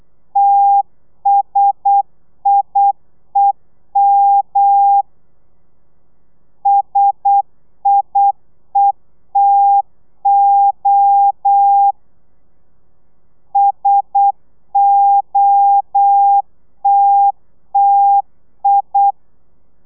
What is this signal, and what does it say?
Morse code, heard as a keyed tone, about 8 wpm, 800 Hz. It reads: TSIEM SIETO SOTTI
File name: telegrafia / i2tfj / Lez1 35